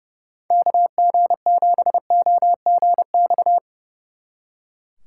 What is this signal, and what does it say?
Morse code, heard as a keyed tone, about 36 wpm, 700 Hz. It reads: KG7OGX
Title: Speed Racing - USA 2x3 Call Signs 30wpm